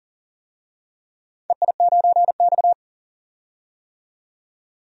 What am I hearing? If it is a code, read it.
Morse code, heard as a keyed tone, about 40 wpm, 700 Hz. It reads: EI9X
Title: DXCC-100 Call Signs - 4 Characters - 40wpm